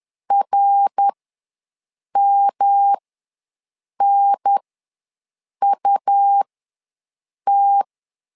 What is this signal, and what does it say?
Morse code, heard as a keyed tone, about 10 wpm, spaced as for 6 wpm, 800 Hz. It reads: RMNUT
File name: LEZIONI / 3 MWRO / 3 MWR ws 10wpm 2151